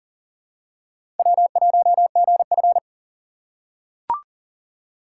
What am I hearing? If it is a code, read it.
Morse code, heard as a keyed tone, about 40 wpm, 700 Hz. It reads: W1GF
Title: USA Call Signs - 4 Characters - 4x Character Spacing - 40wpm